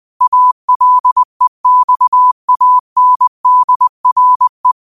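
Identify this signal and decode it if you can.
Morse code, heard as a keyed tone, about 20 wpm, 1000 Hz. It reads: ALEXANDRE
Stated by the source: Morse quote of the day sent at 20 wpm.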